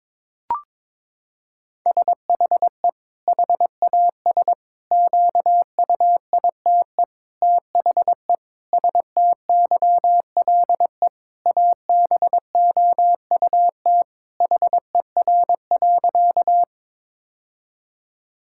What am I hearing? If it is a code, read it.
Morse code, heard as a keyed tone, about 22 wpm, 700 Hz. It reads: SHE HAS QUITE THE STYLE ABOUT HER.
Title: Sentences from Top 1000 Words - Review Part 5 - 22wpm